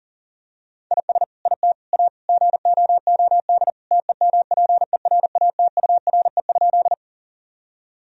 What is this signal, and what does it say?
Morse code, heard as a keyed tone, about 40 wpm, 700 Hz. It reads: IS IT A GOOD TEMPERATURE?
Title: Sentences from Top 1000 Words - Review Part 4 - 40wpm